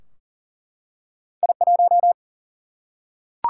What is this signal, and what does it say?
Morse code, heard as a keyed tone, about 42 wpm, 700 Hz. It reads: I1
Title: Single-Letter Single-Number 40wpm